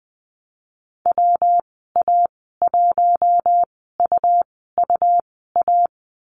Morse code, heard as a keyed tone, about 20 wpm, spaced as for 15 wpm, 700 Hz. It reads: WA1UUA
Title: 147 Lesson 24g - Letter y - Call Signs 20wpm